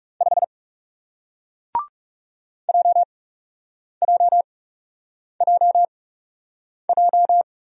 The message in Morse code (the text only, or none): 5 JJJJ